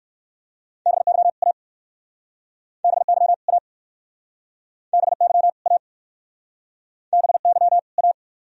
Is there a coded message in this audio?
Yes: Morse code from 0.9 to 8.1 s, heard as a keyed tone, about 60 wpm, 700 Hz. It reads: BY A BY A BY A BY A